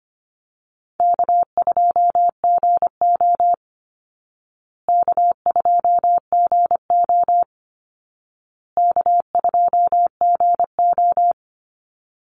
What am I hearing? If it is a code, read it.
Morse code, heard as a keyed tone, about 25 wpm, 700 Hz. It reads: K2GO K2GO K2GO
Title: Call Signs Repeated 2x - 25wpm